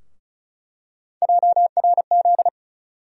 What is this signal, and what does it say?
Morse code, heard as a keyed tone, about 36 wpm, 700 Hz. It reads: JRZ
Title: USA Call Sign Postfixes - 3 Letters 35wpm